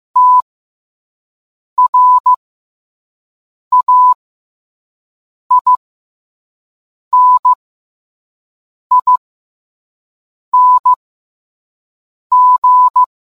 Morse code, heard as a keyed tone, about 15 wpm, spaced as for 5 wpm, 1000 Hz. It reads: TRAINING